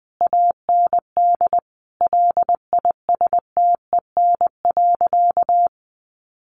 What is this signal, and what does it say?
Morse code, heard as a keyed tone, about 20 wpm, 700 Hz. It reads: AND LISTEN.
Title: Sentences from Top 900 Words 20wpm